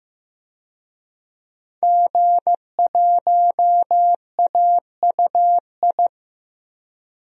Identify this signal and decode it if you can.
Morse code, heard as a keyed tone, about 15 wpm, 700 Hz. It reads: G1AUI
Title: DXCC-100 Call Signs - 5 Characters - Repeated 2x - 15wpm